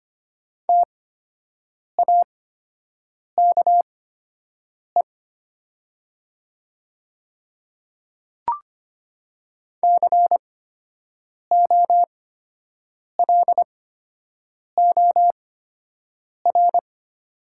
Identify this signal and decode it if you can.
Morse code, heard as a keyed tone, about 25 wpm, spaced as for 7 wpm, 700 Hz. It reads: TAKE COLOR
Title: 8x Character Spacing - Top 300 Words - 25wpm